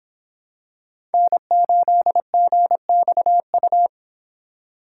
Morse code, heard as a keyed tone, about 25 wpm, 700 Hz. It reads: N8GXU